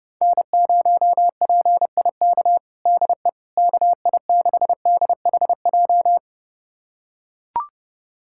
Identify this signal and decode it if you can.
Morse code, heard as a keyed tone, about 32 wpm, 690 Hz. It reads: N0PIK DE KI6DHJ
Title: USA Callsign DE Callsign 30wpm